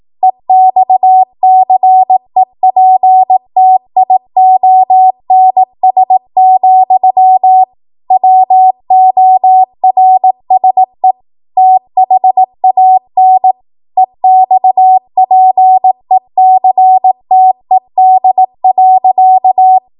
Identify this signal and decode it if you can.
Morse code, heard as a keyed tone, about 18 wpm, 750 Hz. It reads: EXCEPTIONS, WORSE THAN EXPECTED.
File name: Archive / 18 WPM / 18WPM